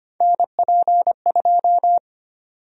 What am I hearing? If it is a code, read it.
Morse code, heard as a keyed tone, about 25 wpm, 700 Hz. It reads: NP2
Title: USA Call Sign Prefixes - Repeated 2x - 25wpm